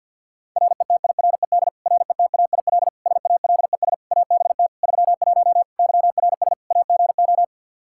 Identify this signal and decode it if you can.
Morse code, heard as a keyed tone, about 53 wpm, 700 Hz. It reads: RETIRED RETAIL SALES ABT 31 YRS AGO
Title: Practice QSO - Exchange 3 - More Info - 50wpm